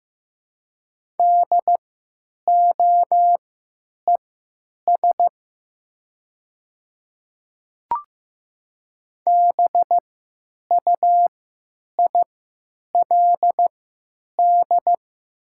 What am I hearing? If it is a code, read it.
Morse code, heard as a keyed tone, about 15 wpm, spaced as for 9 wpm, 700 Hz. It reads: DOES BUILD
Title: 3x Character Spacing - Top 300 Words - 15wpm